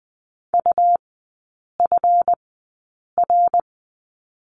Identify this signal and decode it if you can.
Morse code, heard as a keyed tone, about 20 wpm, spaced as for 8 wpm, 700 Hz. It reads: UFR
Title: 205 Lesson 33e - Letter j - Three Characters 20wpm